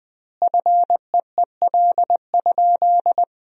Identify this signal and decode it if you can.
Morse code, heard as a keyed tone, about 21 wpm, 700 Hz. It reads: FEEL?
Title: 2x Word-Spacing - Sentences from Top 400 Words - 20wpm